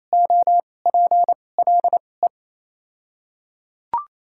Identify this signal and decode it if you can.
Morse code, heard as a keyed tone, about 28 wpm, spaced as for 20 wpm, 700 Hz. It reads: OPLE
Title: 2x Character Spacing - Top 100 Words - 28wpm